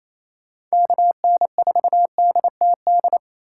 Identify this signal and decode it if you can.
Morse code, heard as a keyed tone, about 30 wpm, 700 Hz. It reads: KN4DTD
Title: Speed Racing - USA 2x3 Call Signs 28wpm